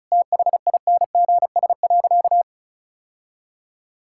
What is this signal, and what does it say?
Morse code, heard as a keyed tone, about 34 wpm, 700 Hz. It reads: THINGS.
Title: Sentences from Top 1000 Words - Review Part 6 - 35wpm